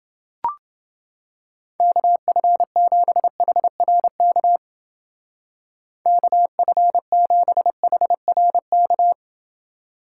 Morse code, heard as a keyed tone, about 29 wpm, 700 Hz. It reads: KF7HRK KF7HRK